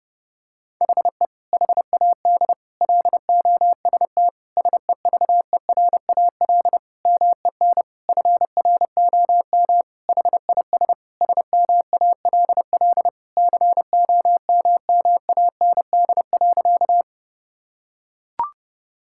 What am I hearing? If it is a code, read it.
Morse code, heard as a keyed tone, about 30 wpm, 700 Hz. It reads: HE HAD LOST SEVERAL MEN FROM HIS SMALL COMMAND.
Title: Sentences from Top 1600 Words 30wpm